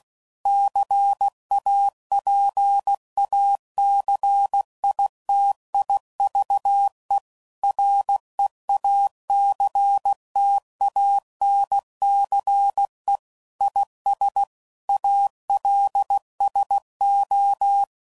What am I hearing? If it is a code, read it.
Morse code, heard as a keyed tone, about 16 wpm, 800 Hz. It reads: CAPACITIVE REACTANCE IS ALSO
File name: CW Code Groups @15WPM / Technical Subjects / XC